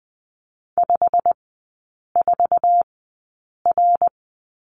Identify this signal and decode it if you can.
Morse code, heard as a keyed tone, about 20 wpm, spaced as for 8 wpm, 700 Hz. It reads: 54R
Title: 159 Lesson 26e - Letter g - Three Characters 20wpm